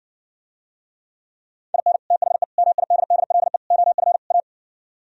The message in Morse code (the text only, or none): IN THE MIDDLE OF A